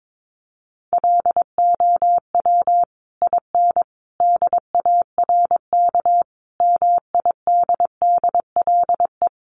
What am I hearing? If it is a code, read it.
Morse code, heard as a keyed tone, about 22 wpm, 700 Hz. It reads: LOW IN DARK MIDDLE
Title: Sentences from Top 900 Words 22wpm